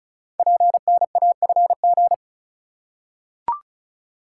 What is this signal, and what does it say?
Morse code, heard as a keyed tone, about 36 wpm, 700 Hz. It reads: PNAFG
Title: Code Groups - 5 Letters - 35wpm